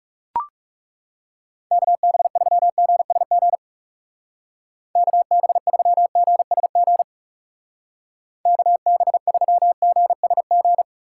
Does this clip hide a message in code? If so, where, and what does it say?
Yes: Morse code from 1.7 to 10.8 s, heard as a keyed tone, about 40 wpm, 700 Hz. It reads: KB3GSG KB3GSG KB3GSG